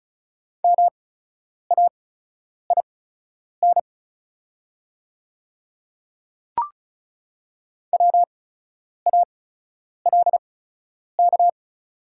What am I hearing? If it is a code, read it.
Morse code, heard as a keyed tone, about 36 wpm, spaced as for 10 wpm, 690 Hz. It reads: MAIN WALK
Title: 8x Character Spacing - Top 500 Words - 35wpm